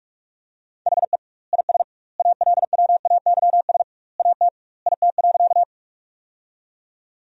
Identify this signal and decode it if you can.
Morse code, heard as a keyed tone, about 46 wpm, 700 Hz. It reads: HE IS ALWAYS AT IT.
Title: 2x Word-Spacing - Sentences from Top 300 Words - 45wpm